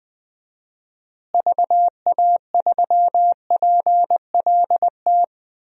Morse code, heard as a keyed tone, about 20 wpm, 700 Hz. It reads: VA3PLT
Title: DXCC-100 Call Signs - 6 Characters - Repeated 3x - 20wpm